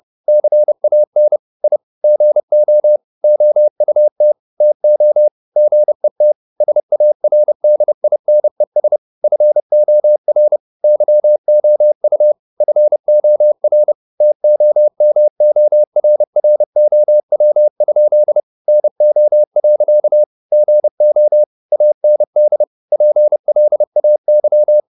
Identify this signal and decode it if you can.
Morse code, heard as a keyed tone, about 30 wpm, 600 Hz. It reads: CAN I GO OUT TO GET SARDINES FOR YOU FOR TOMORROW? NO. GO AND PLAY
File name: the old man and the sea / the old man and the sea 1 30wpm